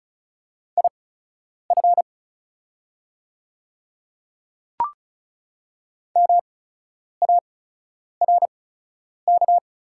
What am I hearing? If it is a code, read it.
Morse code, heard as a keyed tone, about 36 wpm, spaced as for 10 wpm, 690 Hz. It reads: IF MARK
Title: 8x Character Spacing - Top 300 Words - 35wpm